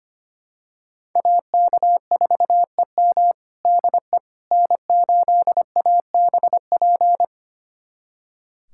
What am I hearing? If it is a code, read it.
Morse code, heard as a keyed tone, about 25 wpm, 700 Hz. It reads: AK4EM DE N8ABP
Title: USA Callsign DE Callsign - 5 Characters - 25wpm